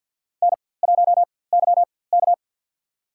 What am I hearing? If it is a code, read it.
Morse code, heard as a keyed tone, about 52 wpm, spaced as for 23 wpm, 700 Hz. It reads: N1YK
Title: USA Call Signs - 4x Character Spacing - 50wpm